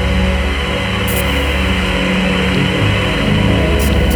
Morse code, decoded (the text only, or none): none